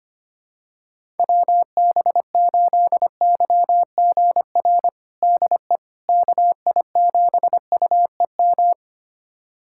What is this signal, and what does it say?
Morse code, heard as a keyed tone, about 25 wpm, 700 Hz. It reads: WB8YGR DE KI7UEM
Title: USA Callsign DE Callsign - 6 Characters - 25wpm